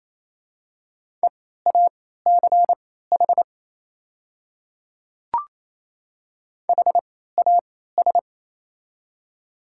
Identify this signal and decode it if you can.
Morse code, heard as a keyed tone, about 28 wpm, spaced as for 16 wpm, 690 Hz. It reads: EACH HAS
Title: 3x Character Spacing - Top 100 Words - 28wpm